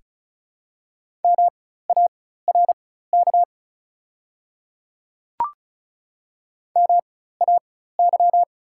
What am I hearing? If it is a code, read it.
Morse code, heard as a keyed tone, about 36 wpm, spaced as for 16 wpm, 700 Hz. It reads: MARK MAY
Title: 4x Character Spacing - Top 300 Words - 35wpm